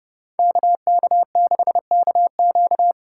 Morse code, heard as a keyed tone, about 30 wpm, 700 Hz. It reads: KK6KQ